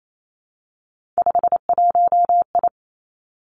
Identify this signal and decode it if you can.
Morse code, heard as a keyed tone, about 28 wpm, 700 Hz. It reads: 51I
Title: Three Challenging Letter-Number Combinations 28wpm